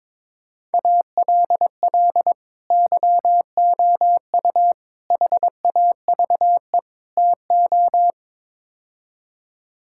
Morse code, heard as a keyed tone, about 22 wpm, 700 Hz. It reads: ALL YOU HAVE TO
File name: practice / 22wpm / Build A Phrase - 5-Word Phrases - 22wpm